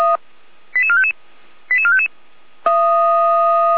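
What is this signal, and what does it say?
Five-tone selective call (ZVEI), heard as quick runs of five tones, tones of about 70 ms. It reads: 88455 88455
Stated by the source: Received on an SDR as a ZVEI five-tone call.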